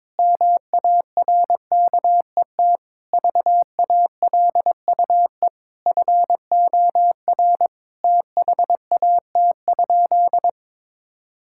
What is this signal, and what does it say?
Morse code, heard as a keyed tone, about 23 wpm, 700 Hz. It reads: MARKET VALUE FOR THAT?
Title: Sentences from Top 1000 Words - Review Part 6 - 22wpm